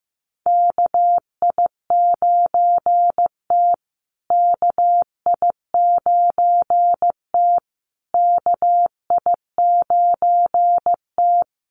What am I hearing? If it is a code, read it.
Morse code, heard as a keyed tone, about 15 wpm, 700 Hz. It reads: KI9T KI9T KI9T
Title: USA Call Signs Repeated 3x 15wpm